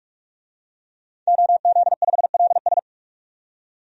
Morse code, heard as a keyed tone, about 43 wpm, 700 Hz. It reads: OZ5LS
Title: DXCC-100 Call Signs - 5 Characters - 45wpm